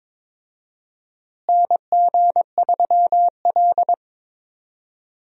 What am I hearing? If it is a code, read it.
Morse code, heard as a keyed tone, about 21 wpm, 700 Hz. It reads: NG3L
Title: USA Call Signs - 4 Characters - Repeated 2x - 22wpm